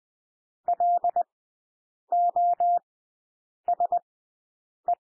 Morse code, heard as a keyed tone, about 20 wpm, spaced as for 8 wpm, 700 Hz. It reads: LOSE
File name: cwmp3 / W1 SOUND WORDS 20 wpm